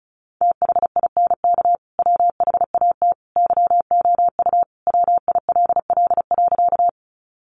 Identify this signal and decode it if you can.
Morse code, heard as a keyed tone, about 34 wpm, 700 Hz. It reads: THINK WHAT YOU WILL.
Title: Sentences from Top 200 Words 35wpm